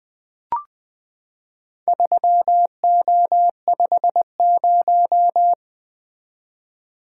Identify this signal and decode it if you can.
Morse code, heard as a keyed tone, about 20 wpm, 700 Hz. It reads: 3O50
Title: Four Letter-Number Combinations - 20wpm